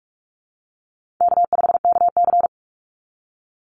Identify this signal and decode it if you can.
Morse code, heard as a keyed tone, about 46 wpm, 700 Hz. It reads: K5KC
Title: Easy Call Signs - 45wpm